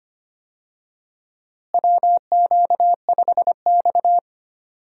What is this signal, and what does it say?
Morse code, heard as a keyed tone, about 25 wpm, 700 Hz. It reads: WQ5X
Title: USA Call Signs - 4 Characters - Repeated 2x - 25wpm